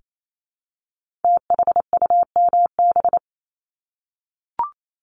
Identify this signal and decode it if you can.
Morse code, heard as a keyed tone, about 28 wpm, 700 Hz. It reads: THUMB
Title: Multi-letter Phonemes Repeated 3x 28wpm